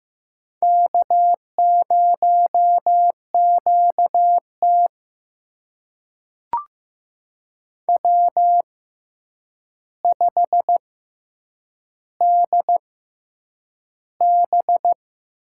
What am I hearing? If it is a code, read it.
Morse code, heard as a keyed tone, about 15 wpm, spaced as for 8 wpm, 700 Hz. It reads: K0QT W5DB
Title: USA Call Signs - 4 Characters - 6x Character Spacing - 15wpm